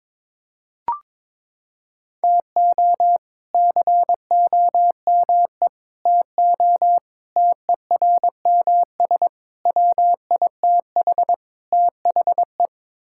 Morse code, heard as a keyed tone, about 22 wpm, 700 Hz. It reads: TO COME TO TERMS WITH THE
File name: practice / 22wpm / Sets of 6 Words - 22wpm